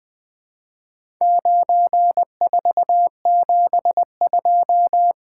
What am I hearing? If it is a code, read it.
Morse code, heard as a keyed tone, about 20 wpm, 700 Hz. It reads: 9472
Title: Four Letter-Number Combinations - 20wpm